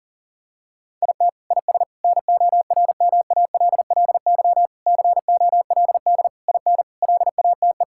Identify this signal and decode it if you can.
Morse code, heard as a keyed tone, about 40 wpm, 700 Hz. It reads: IT IS NORMALLY COLD IN LATE
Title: Sentences from Top 1000 Words - Review Part 6 - 40wpm